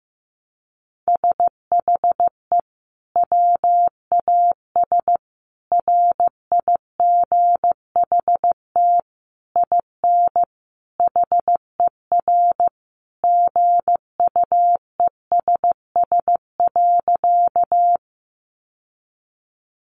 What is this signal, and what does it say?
Morse code, heard as a keyed tone, about 15 wpm, 700 Hz. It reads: SHE WAS RIGHT IN HER GUESS.
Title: Sentences from Top 1500 Words 15wpm